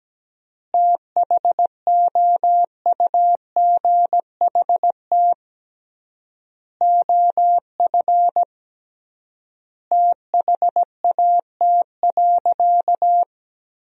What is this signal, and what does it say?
Morse code, heard as a keyed tone, about 17 wpm, 700 Hz. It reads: THOUGHT OF THAT.
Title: 3x Word-Spacing - Sentences from Top 300 Words - 17wpm